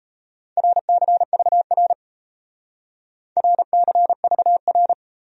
Morse code, heard as a keyed tone, about 34 wpm, 700 Hz. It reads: RCVR RCVR
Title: Speed-Racing Abbreviations - 28wpm